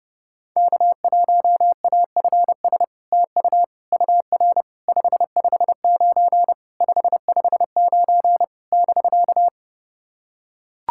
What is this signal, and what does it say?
Morse code, heard as a keyed tone, about 30 wpm, 700 Hz. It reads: K1AFS TU UR 559 559 <BK>
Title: SOTA QSO Practice 30wpm